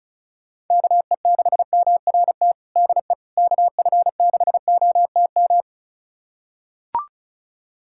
Morse code, heard as a keyed tone, about 36 wpm, 700 Hz. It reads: KE6MRT DE KF6OTM
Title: USA Callsign DE Callsign - 6 Characters - 35wpm